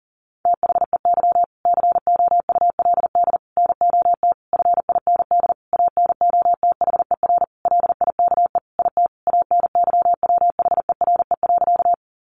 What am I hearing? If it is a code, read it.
Morse code, heard as a keyed tone, about 40 wpm, 700 Hz. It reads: THEY COULD NOT FIND ANOTHER LIKE IT ANYWHERE.
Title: Sentences from Top 1400 Words 40wpm